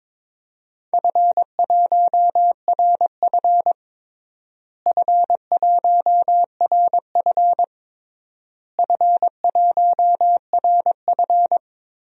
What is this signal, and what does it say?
Morse code, heard as a keyed tone, about 22 wpm, 700 Hz. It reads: F1RF F1RF F1RF
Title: DXCC-100 Call Signs - 4 Characters - Repeated 3x - 22wpm